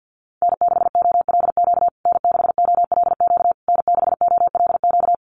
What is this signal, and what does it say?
Morse code, heard as a keyed tone, about 50 wpm, 700 Hz. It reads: N6OLQ N6OLQ N6OLQ
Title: USA Call Signs Repeated 3x 50wpm